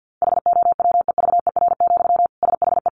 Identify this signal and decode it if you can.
Morse code, heard as a keyed tone, about 50 wpm, 700 Hz. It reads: HOWEVER, SHE
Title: Sentences from Top 700 Words 50wpm